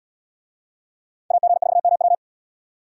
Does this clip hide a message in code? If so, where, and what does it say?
Yes: Morse code from 1.3 to 2.1 s, heard as a keyed tone, about 70 wpm, 700 Hz. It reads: AG4MW